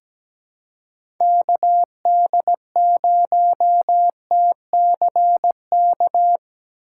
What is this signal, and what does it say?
Morse code, heard as a keyed tone, about 17 wpm, 700 Hz. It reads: KD0TCK
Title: USA Call Signs - 3x Character Spacing - 17wpm